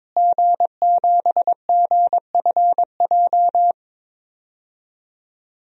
Morse code, heard as a keyed tone, about 22 wpm, 700 Hz. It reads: G7GFJ
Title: DXCC-100 Call Signs - 5 Characters - Repeated 3x - 22wpm